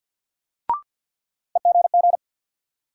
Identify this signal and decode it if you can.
Morse code, heard as a keyed tone, about 47 wpm, 700 Hz. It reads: EGG
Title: Sentences from Top 1000 Words - Review Part 7 - New Word Drill - 50wpm